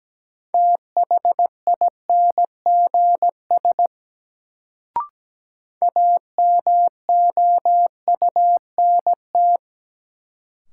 Morse code, heard as a keyed tone, about 17 wpm, 700 Hz. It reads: THINGS AMOUNT
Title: Sets of 3 Words - New Word Drill - 17wpm